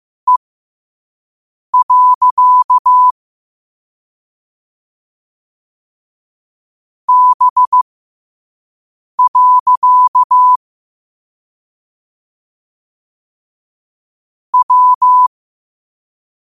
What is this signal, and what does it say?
Morse code, heard as a keyed tone, about 15 wpm, spaced as for 5 wpm, 1000 Hz. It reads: E. B. W
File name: qotdfiles / 5wpm